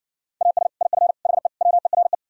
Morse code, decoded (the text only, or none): AS IF HE WERE